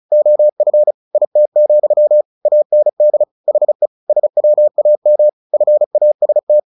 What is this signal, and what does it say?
Morse code, heard as a keyed tone, about 34 wpm, 600 Hz. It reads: OF IT, AND HE SWAM FAST
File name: the old man and the sea 2 35wpm